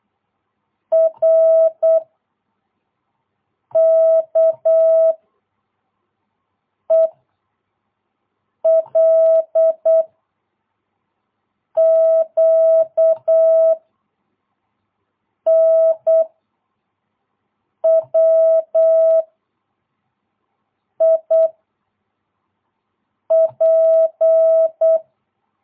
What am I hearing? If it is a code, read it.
Morse code, heard as a keyed tone, about 8 wpm, 660 Hz. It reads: R K E L Q N W I P
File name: jh8cbh / 2022cw